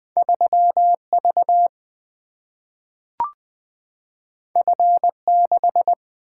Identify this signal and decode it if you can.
Morse code, heard as a keyed tone, about 20 wpm, 700 Hz. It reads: 3 V F 6
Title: Single-Letter Single-Number 20wpm